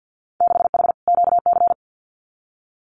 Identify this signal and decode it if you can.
Morse code, heard as a keyed tone, about 55 wpm, 700 Hz. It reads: 6H QC
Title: Winter Field Day Exchanges Encore 50wpm